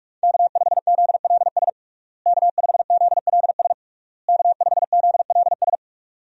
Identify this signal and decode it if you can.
Morse code, heard as a keyed tone, about 46 wpm, 700 Hz. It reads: K5ZLS K5ZLS K5ZLS
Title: USA Call Signs - 5 Characters - Repeated 3x - 45wpm